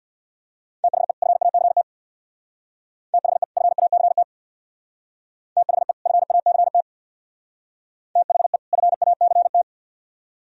Morse code, heard as a keyed tone, about 64 wpm, 700 Hz. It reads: THE FACT THE FACT THE FACT THE FACT